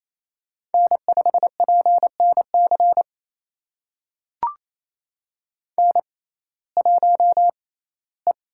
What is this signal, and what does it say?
Morse code, heard as a keyed tone, about 28 wpm, spaced as for 17 wpm, 700 Hz. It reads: N5PNC N1E
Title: USA Call Signs - 5 Characters - 6x Character Spacing - 28wpm